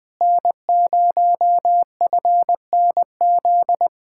Morse code, heard as a keyed tone, about 20 wpm, 700 Hz. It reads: N0FNZ
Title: Speed-Racing Common QSO Phrases - 17wpm